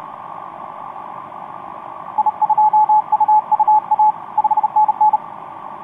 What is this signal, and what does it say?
Morse code, heard as a keyed tone, about 30 wpm, 860 Hz, with radio noise. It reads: I2UUA 5NN
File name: t46rrc i2uua etc 12m cw 18feb2013 1528